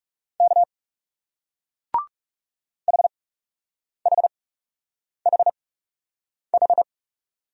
Morse code, heard as a keyed tone, about 41 wpm, spaced as for 8 wpm, 700 Hz. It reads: K HHHH